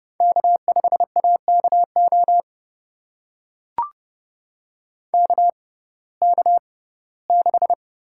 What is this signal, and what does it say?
Morse code, heard as a keyed tone, about 31 wpm, spaced as for 19 wpm, 700 Hz. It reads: K5AKO KK6
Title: USA Call Signs - 5 Characters - 6x Character Spacing - 30wpm